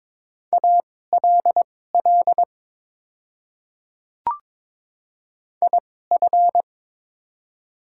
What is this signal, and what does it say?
Morse code, heard as a keyed tone, about 22 wpm, spaced as for 16 wpm, 700 Hz. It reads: ALL IF
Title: 2x Character Spacing - Top 100 Words - 22wpm